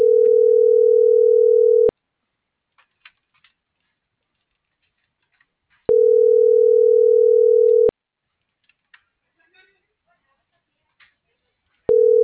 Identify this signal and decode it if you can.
Ringback tone (North America), 440 + 480 Hz.